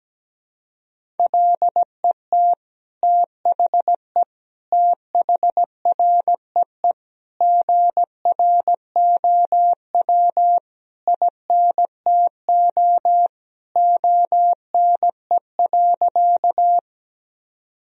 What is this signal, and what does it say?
Morse code, heard as a keyed tone, about 17 wpm, 700 Hz. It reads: LET THE THREE GROW INTO ONE.
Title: Sentences from Top 1000 Words - Review Part 1 - 17wpm